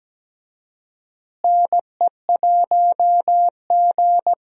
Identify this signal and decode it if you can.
Morse code, heard as a keyed tone, about 17 wpm, 700 Hz. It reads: NE1G
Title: USA Call Signs - 4 Characters - Repeated 2x - 17wpm